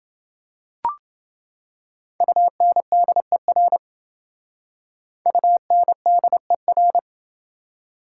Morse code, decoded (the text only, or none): UNDER UNDER